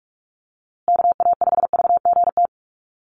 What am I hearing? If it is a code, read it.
Morse code, heard as a keyed tone, about 44 wpm, 700 Hz. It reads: KA5VGT
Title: USA Call Signs Repeated 3x 45wpm